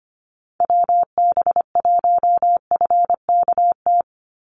Morse code, heard as a keyed tone, about 25 wpm, 700 Hz. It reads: WB1FKT
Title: USA Call Signs 25wpm